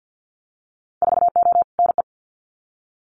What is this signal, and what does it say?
Morse code, heard as a keyed tone, about 50 wpm, 700 Hz. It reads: 4O NE
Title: Winter Field Day Exchanges Encore 50wpm